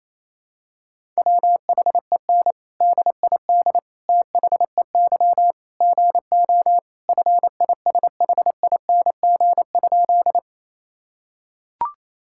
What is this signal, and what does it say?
Morse code, heard as a keyed tone, about 28 wpm, 700 Hz. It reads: WHEN DID THEY GO FISHING?
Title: Sentences from Top 1000 Words - Review Part 5 - 28wpm